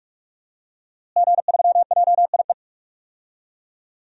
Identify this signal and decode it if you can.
Morse code, heard as a keyed tone, about 46 wpm, 700 Hz. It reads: G3JIE
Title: DXCC-100 Call Signs - 5 Characters - Repeated 3x - 45wpm